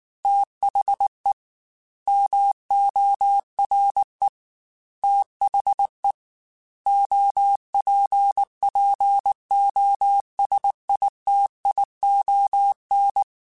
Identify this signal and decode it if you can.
Morse code, heard as a keyed tone, about 19 wpm, 800 Hz. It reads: THE MORE THE OPPOSITION
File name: Technical Subjects / Technical Subjects-mp3s / XC @18